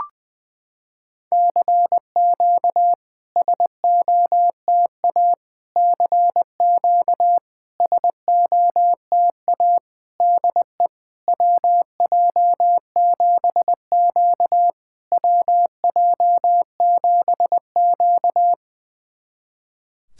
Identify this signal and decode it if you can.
Morse code, heard as a keyed tone, about 21 wpm, 690 Hz. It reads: CQ SOTA CQ SOTA DE WJ7Q WJ7Q
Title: SOTA QSO Practice 20wpm